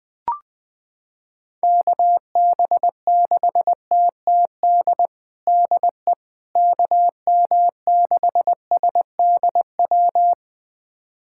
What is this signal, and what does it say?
Morse code, heard as a keyed tone, about 21 wpm, 700 Hz. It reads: KB6TTD DE KM6SDW
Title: USA Callsign DE Callsign - 6 Characters - 20wpm